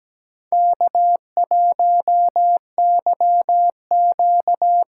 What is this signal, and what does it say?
Morse code, heard as a keyed tone, about 17 wpm, 700 Hz. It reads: K1YQ
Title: USA Call Signs - 4 Characters - 4x Character Spacing - 17wpm